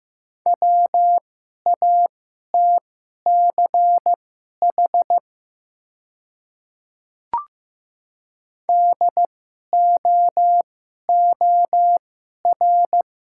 Morse code, heard as a keyed tone, about 15 wpm, spaced as for 11 wpm, 700 Hz. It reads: WATCH DOOR
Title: 2x Character Spacing - Top 300 Words - 15wpm